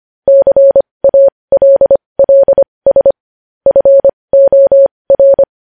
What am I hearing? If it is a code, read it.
Morse code, heard as a keyed tone, about 25 wpm, spaced as for 20 wpm, 550 Hz. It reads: CALLS FOR
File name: podcast / News headlines-1x-25x20